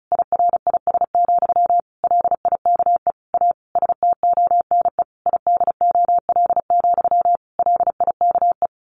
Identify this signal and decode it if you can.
Morse code, heard as a keyed tone, about 34 wpm, 700 Hz. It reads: IRIS, LIKE A STONE IDOL, LIKE
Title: A Sound Of Thunder 35wpm